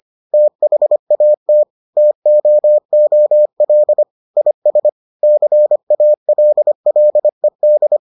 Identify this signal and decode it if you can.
Morse code, heard as a keyed tone, about 25 wpm, 600 Hz. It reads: THAT TOOL IS CALLED